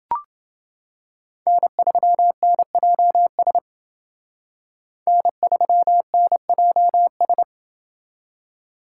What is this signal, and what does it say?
Morse code, heard as a keyed tone, about 28 wpm, 700 Hz. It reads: N3NJS N3NJS